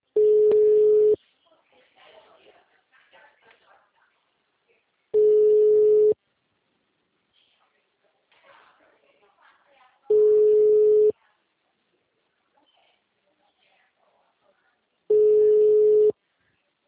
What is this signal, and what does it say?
Ringback tone (Europe, 425 Hz), with speech.
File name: out- -2010- - - .1163